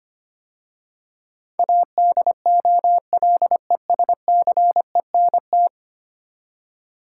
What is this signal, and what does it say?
Morse code, heard as a keyed tone, about 25 wpm, 700 Hz. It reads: ADOLESCENT
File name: 10 Letter Words - First 3 Letters Given - 25wpm